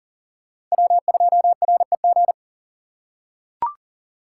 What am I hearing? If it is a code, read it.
Morse code, heard as a keyed tone, about 40 wpm, 700 Hz. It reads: W2REG
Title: USA Call Signs - 5 Characters - 3x Character Spacing - 40wpm